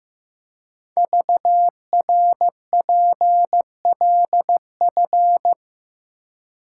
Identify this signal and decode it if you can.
Morse code, heard as a keyed tone, about 15 wpm, 700 Hz. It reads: VRPLF
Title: Code Groups - 5 Letters - 15wpm